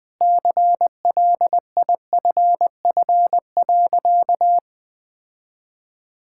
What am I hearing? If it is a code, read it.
Morse code, heard as a keyed tone, about 20 wpm, 700 Hz. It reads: CLIFF.
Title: Sentences with 5-Word Phrases - 20wpm